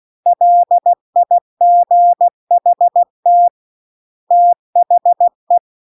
Morse code, heard as a keyed tone, about 16 wpm, 700 Hz. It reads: LIGHT THE
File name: CW-The-Big-Twoheaded-River-2-16WPM